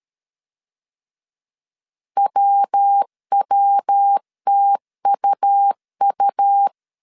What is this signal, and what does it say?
Morse code, heard as a keyed tone, about 13 wpm, 800 Hz. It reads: WWTUU
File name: CW / DET / 1 3 1500C S B 12wpm 2828